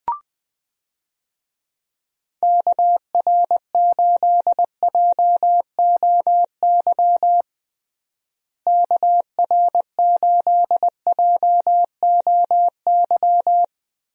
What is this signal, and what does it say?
Morse code, heard as a keyed tone, about 20 wpm, 700 Hz. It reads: KR8JOY KR8JOY
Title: USA Call Signs - 6 Characters - Repeated 3x - 20wpm